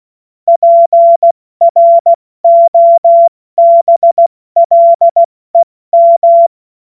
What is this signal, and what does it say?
Morse code, heard as a keyed tone, about 16 wpm, 680 Hz. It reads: PROBLEM